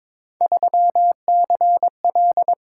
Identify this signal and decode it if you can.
Morse code, heard as a keyed tone, about 22 wpm, 700 Hz. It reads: 3CL
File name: practice / 22wpm / Three Letter-Number Combinations - 22wpm